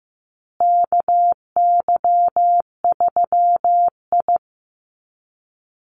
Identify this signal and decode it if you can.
Morse code, heard as a keyed tone, about 14 wpm, 700 Hz. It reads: KY3I
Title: USA Call Signs Repeated 2x 15wpm